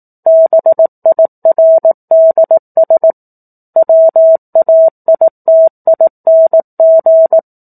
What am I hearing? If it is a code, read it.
Morse code, heard as a keyed tone, about 18 wpm, 650 Hz. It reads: BIRDS WAITING
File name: 04-Chapter-04 18 WPM